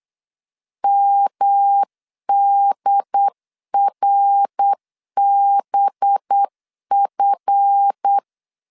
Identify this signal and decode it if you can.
Morse code, heard as a keyed tone, about 8 wpm, 800 Hz. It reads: MDRBF
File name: LEZIONI / 7 GF9Z / 7 GF 8wpm 3108